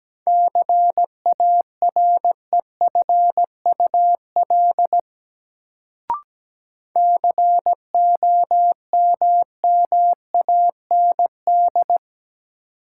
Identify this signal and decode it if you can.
Morse code, heard as a keyed tone, about 17 wpm, 700 Hz. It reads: CAREFUL COMMAND
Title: Sentences from Top 1600 Words - New Word Drill - 17wpm